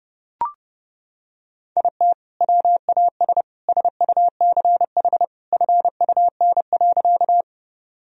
Morse code, heard as a keyed tone, about 30 wpm, 700 Hz. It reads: IT WAS SUCH FUN.
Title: Sentences from Top 1000 Words - Review Part 4 - 30wpm